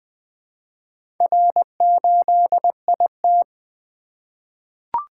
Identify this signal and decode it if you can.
Morse code, heard as a keyed tone, about 20 wpm, 700 Hz. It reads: R8IT
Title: Four Letter-Number Combinations - 20wpm